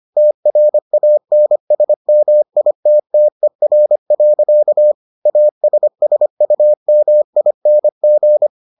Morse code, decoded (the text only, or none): TRANSMITTER. ASSUMING